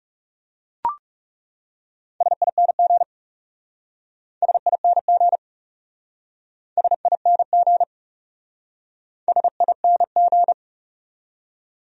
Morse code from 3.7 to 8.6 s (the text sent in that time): SING SING